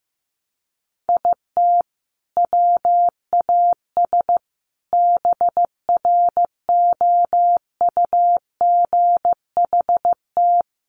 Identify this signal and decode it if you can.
Morse code, heard as a keyed tone, about 15 wpm, 700 Hz. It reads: IT WAS BROUGHT
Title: Sentences from Top 500 Words 15wpm